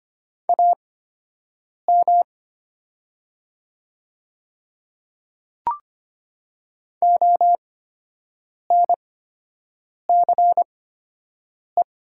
Morse code, heard as a keyed tone, about 25 wpm, spaced as for 7 wpm, 700 Hz. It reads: AM ONCE
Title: 8x Character Spacing - Top 500 Words - 25wpm